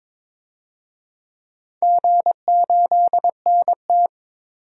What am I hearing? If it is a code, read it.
Morse code, heard as a keyed tone, about 22 wpm, 700 Hz. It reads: G8NT
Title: DXCC-100 Call Signs - 4 Characters - 22wpm